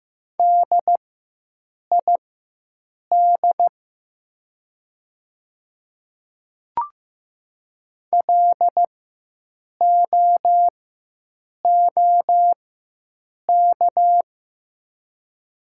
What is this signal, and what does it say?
Morse code, heard as a keyed tone, about 15 wpm, spaced as for 7 wpm, 690 Hz. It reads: DID LOOK
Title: 4x Character Spacing - Top 500 Words - 15wpm